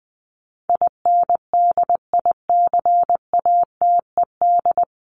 Morse code, heard as a keyed tone, about 20 wpm, 700 Hz. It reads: INDICATED
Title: ED Suffix - Top 100 Words with Base Words 20wpm